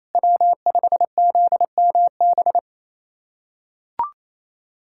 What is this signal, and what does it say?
Morse code, heard as a keyed tone, about 28 wpm, 700 Hz. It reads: W5ZMB
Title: USA Callsign DE Callsign - 5 Characters - 28wpm